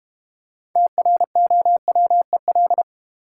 Morse code, heard as a keyed tone, about 32 wpm, 700 Hz. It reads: TROWEL